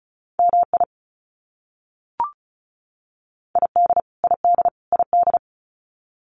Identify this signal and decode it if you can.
Morse code, heard as a keyed tone, about 34 wpm, 700 Hz. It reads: MI ID ID ID
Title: US States Repeated 3x 35wpm